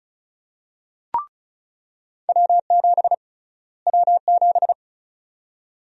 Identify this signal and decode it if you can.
Morse code, heard as a keyed tone, about 34 wpm, 700 Hz. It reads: W7 W7
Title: USA Call Sign Prefixes - Repeated 2x - 35wpm